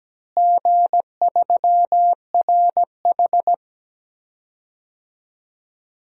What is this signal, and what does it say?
Morse code, heard as a keyed tone, about 17 wpm, 700 Hz. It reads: G3RH
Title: DXCC-100 Call Signs - 4 Characters - Repeated 3x - 17wpm